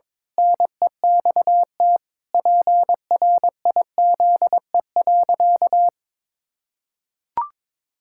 Morse code, heard as a keyed tone, about 22 wpm, 700 Hz. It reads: NEXT PRIZE.
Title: Pangram Challenge v2 - 22wpm